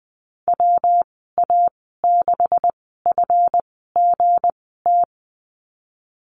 Morse code, heard as a keyed tone, about 19 wpm, spaced as for 14 wpm, 700 Hz. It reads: WA6FGT
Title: 179 Lesson 29g - Slash - Call Signs 20wpm